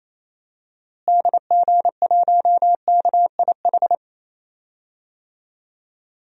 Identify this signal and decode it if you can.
Morse code, heard as a keyed tone, about 28 wpm, 700 Hz. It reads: DG1KIH
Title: DXCC-100 Call Signs - 6 Characters - Repeated 3x - 28wpm